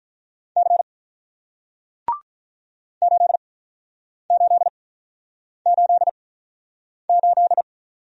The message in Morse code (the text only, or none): C 8888